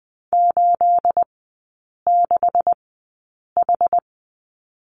Morse code, heard as a keyed tone, about 20 wpm, spaced as for 8 wpm, 690 Hz. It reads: 86H
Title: 212 Lesson 34e - Number 8 - Three Characters 20wpm